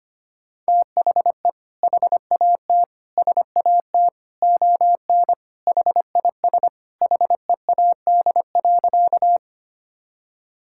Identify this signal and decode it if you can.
Morse code, heard as a keyed tone, about 25 wpm, 700 Hz. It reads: THE HAT SAT ON HIS HEAD.
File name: Sentences with 2-Word Phrases - 25wpm